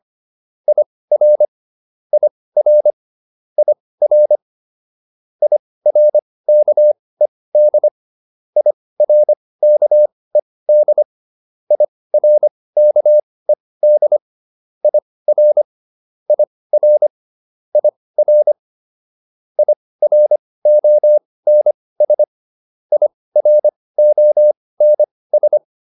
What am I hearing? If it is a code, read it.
Morse code, heard as a keyed tone, about 25 wpm, spaced as for 18 wpm, 600 Hz. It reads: IR IR IR IRKED IRKED IRKED IR IR IR IRONS IRONS